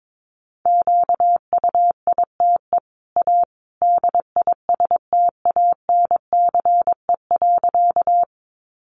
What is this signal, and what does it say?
Morse code, heard as a keyed tone, about 22 wpm, 700 Hz. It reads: QUITE A DISTANCE.
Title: Sentences from Top 1200 Words 22wpm